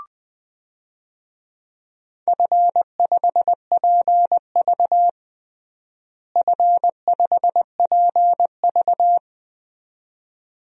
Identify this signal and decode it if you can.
Morse code, heard as a keyed tone, about 21 wpm, 700 Hz. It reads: F5PV F5PV
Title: DXCC-100 Call Signs - 4 Characters - Repeated 2x - 20wpm